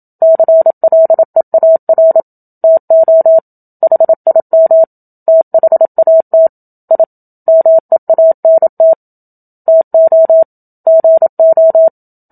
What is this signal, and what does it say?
Morse code, heard as a keyed tone, about 27 wpm, 650 Hz. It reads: CLEAR TO HIM THAT I MEANT TO GO
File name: Book Two / 01-Chapter 1 - 27 WPM